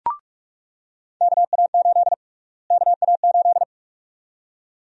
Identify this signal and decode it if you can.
Morse code, heard as a keyed tone, about 42 wpm, 700 Hz. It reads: KA8 KA8
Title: USA Call Sign Prefixes - Repeated 2x - 45wpm